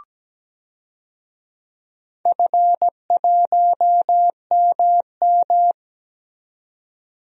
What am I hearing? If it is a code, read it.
Morse code, heard as a keyed tone, about 17 wpm, 700 Hz. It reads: F1MM
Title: DXCC-100 Call Signs - 4 Characters - 17wpm